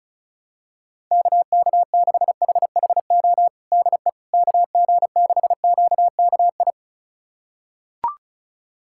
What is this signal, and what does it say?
Morse code, heard as a keyed tone, about 36 wpm, 700 Hz. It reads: KK6HHO DE KG6QKI
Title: USA Callsign DE Callsign 35wpm